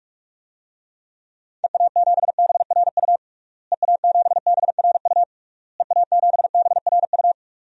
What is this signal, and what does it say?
Morse code, heard as a keyed tone, about 47 wpm, 700 Hz. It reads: EA7BRU EA7BRU EA7BRU
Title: DXCC-100 Call Signs - 6 Characters - Repeated 3x - 45wpm